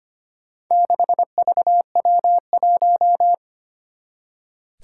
Morse code, heard as a keyed tone, about 25 wpm, 700 Hz. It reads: 6VW1
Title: Four Challenging Letter-Number Combinations 25wpm